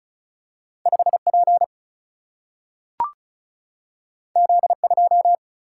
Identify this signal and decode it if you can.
Morse code, heard as a keyed tone, about 36 wpm, 700 Hz. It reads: 5P Z2
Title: Single-Letter Single-Number 35wpm